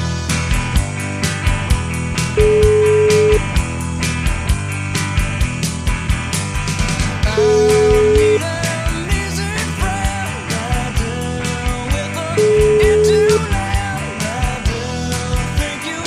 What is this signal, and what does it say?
Ringback tone (Europe, 425 Hz), with noise.